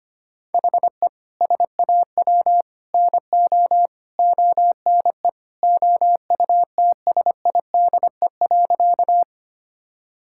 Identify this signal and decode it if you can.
Morse code, heard as a keyed tone, about 25 wpm, 700 Hz. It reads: HE SAW NO ONE OUTSIDE.
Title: Sentences from Top 1000 Words - Review Part 6 - 25wpm